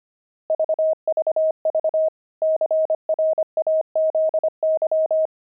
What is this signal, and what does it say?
Morse code, heard as a keyed tone, about 25 wpm, 630 Hz. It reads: VVV CRAZY
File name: pangram-fs-25-cs-25-opal